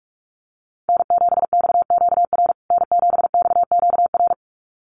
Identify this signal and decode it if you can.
Morse code, heard as a keyed tone, about 44 wpm, 690 Hz. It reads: N7XQR N7XQR
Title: USA Call Signs Repeated 2x 45wpm